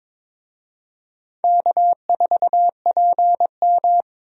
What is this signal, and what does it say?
Morse code, heard as a keyed tone, about 22 wpm, 700 Hz. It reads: K4PM
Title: USA Call Signs - 4 Characters - Repeated 3x - 22wpm